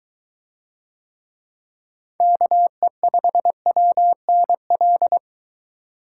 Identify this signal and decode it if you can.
Morse code, heard as a keyed tone, about 21 wpm, 700 Hz. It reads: KE5WNL